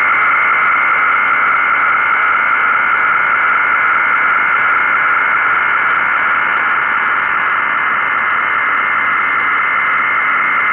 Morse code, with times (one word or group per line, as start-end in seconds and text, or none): none